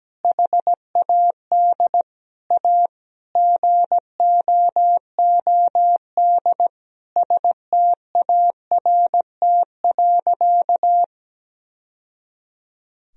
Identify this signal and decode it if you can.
Morse code, heard as a keyed tone, about 17 wpm, 700 Hz. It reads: HAD A GOOD START.
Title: Sentences from Top 1000 Words - Review Part 1 - 17wpm